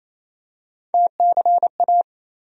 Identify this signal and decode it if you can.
Morse code, heard as a keyed tone, about 28 wpm, 700 Hz. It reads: TCA
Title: USA Call Sign Postfixes - 3 Letters 28wpm